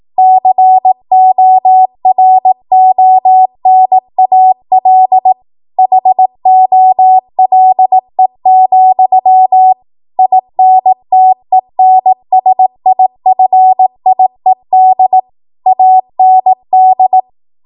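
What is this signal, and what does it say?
Morse code, heard as a keyed tone, about 18 wpm, 750 Hz. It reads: CORONAL HOLE, INTENSIFIED AND
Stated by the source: Code practice at 18 WPM, a transition run bridging 15 to 20 WPM.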